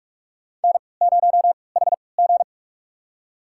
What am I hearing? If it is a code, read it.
Morse code, heard as a keyed tone, about 44 wpm, spaced as for 25 wpm, 690 Hz. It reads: N0HG
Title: USA Call Signs - 4 Characters - 3x Character Spacing - 45wpm